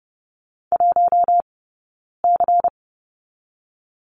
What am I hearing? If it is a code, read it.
Morse code, heard as a keyed tone, about 30 wpm, 700 Hz. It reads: 1 C
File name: Lesson 17d - Letter u - Two Characters 30wpm